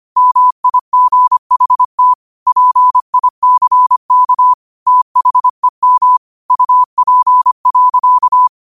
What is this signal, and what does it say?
Morse code, heard as a keyed tone, about 25 wpm, 1000 Hz. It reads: MIGHT PICK THEM UP.